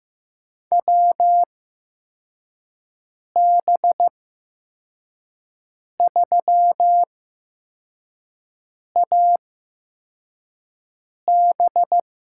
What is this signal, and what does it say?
Morse code, heard as a keyed tone, about 15 wpm, spaced as for 4 wpm, 700 Hz. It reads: WB3AB
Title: USA Call Signs - 6 Characters - 8x Character Spacing - 15wpm